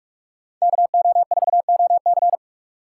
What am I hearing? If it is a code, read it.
Morse code, heard as a keyed tone, about 44 wpm, 700 Hz. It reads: KO4OC
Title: USA Call Signs - 5 Characters - 8x Character Spacing - 45wpm